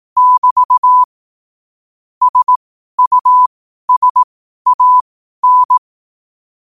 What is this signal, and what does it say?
Morse code, heard as a keyed tone, about 18 wpm, spaced as for 12 wpm, 1000 Hz. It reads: <BT> SUSAN
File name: qotdfiles / 12wpm